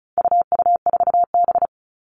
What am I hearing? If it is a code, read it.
Morse code, heard as a keyed tone, about 34 wpm, 700 Hz. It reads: UU4B
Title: All Cut Numbers - 4-Digit 35wpm